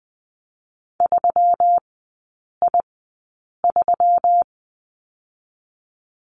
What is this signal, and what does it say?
Morse code, heard as a keyed tone, about 20 wpm, spaced as for 8 wpm, 700 Hz. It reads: 3I3
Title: 121 Lesson 20e - Number 3 - Three Characters 20wpm